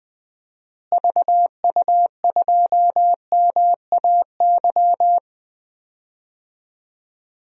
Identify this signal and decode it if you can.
Morse code, heard as a keyed tone, about 20 wpm, 700 Hz. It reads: VU2MAY
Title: DXCC-100 Call Signs - 6 Characters - Repeated 2x - 20wpm